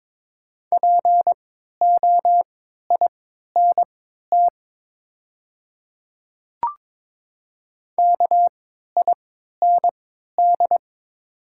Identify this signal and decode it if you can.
Morse code, heard as a keyed tone, about 22 wpm, spaced as for 12 wpm, 700 Hz. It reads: POINT KIND
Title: 3x Character Spacing - Top 300 Words - 22wpm